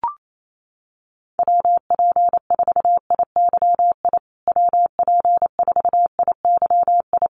Morse code, heard as a keyed tone, about 28 wpm, 700 Hz. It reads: WP4IYI WP4IYI
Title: USA Call Signs Repeated 3x 28wpm